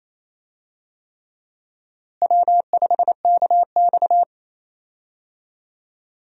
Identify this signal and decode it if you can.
Morse code, heard as a keyed tone, about 28 wpm, 700 Hz. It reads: W5KX
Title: USA Call Signs - 4 Characters - Repeated 3x - 28wpm